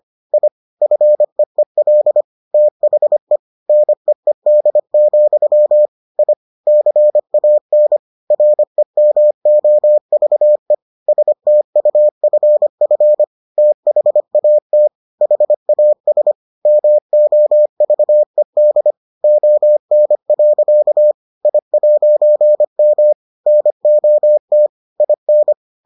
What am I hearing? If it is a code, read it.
Morse code, heard as a keyed tone, about 25 wpm, 600 Hz. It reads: I FEEL THE NEED, I CAN REMOVE STUFF THAT HAS MOVED ON. I'M NOT IN